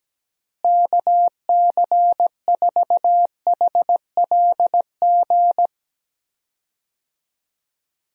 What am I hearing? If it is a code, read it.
Morse code, heard as a keyed tone, about 17 wpm, 700 Hz. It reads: KC4HLG
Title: USA Call Signs - 6 Characters - Repeated 3x - 17wpm